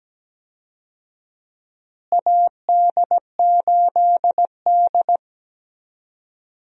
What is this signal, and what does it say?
Morse code, heard as a keyed tone, about 17 wpm, 700 Hz. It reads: AD8D
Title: USA Call Signs - 4 Characters - Repeated 2x - 17wpm